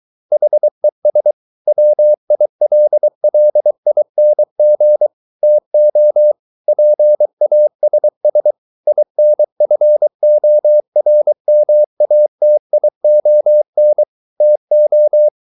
Morse code, heard as a keyed tone, about 23 wpm, 600 Hz. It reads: HES WILLING TO PASS INFORMATION TO